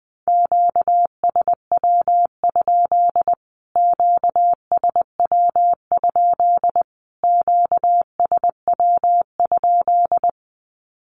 Morse code, heard as a keyed tone, about 20 wpm, 700 Hz. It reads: QSW? QSW? QSW?
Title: Q-Codes Repeated 3x 20wpm